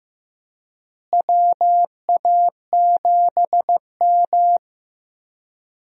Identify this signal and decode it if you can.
Morse code, heard as a keyed tone, about 15 wpm, 700 Hz. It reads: WA7M
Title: USA Call Signs - 4 Characters - Repeated 2x - 15wpm